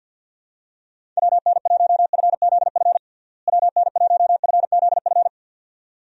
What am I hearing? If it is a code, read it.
Morse code, heard as a keyed tone, about 52 wpm, 700 Hz. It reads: WN1FZF WN1FZF
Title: USA Call Signs - 6 Characters - Repeated 2x - 50wpm